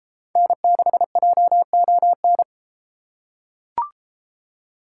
Morse code, heard as a keyed tone, about 31 wpm, 700 Hz. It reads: N6JON